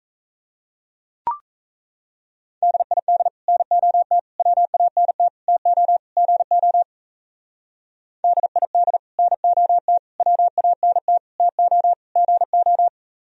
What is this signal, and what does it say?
Morse code, heard as a keyed tone, about 32 wpm, 700 Hz. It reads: DID NOT WANT TO GO DID NOT WANT TO GO
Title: Speed-Racing - Sets of 5 Words 28wpm